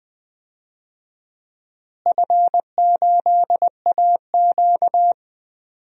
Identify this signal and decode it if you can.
Morse code, heard as a keyed tone, about 20 wpm, 700 Hz. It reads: F8AQ
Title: DXCC-100 Call Signs - 4 Characters - Repeated 2x - 20wpm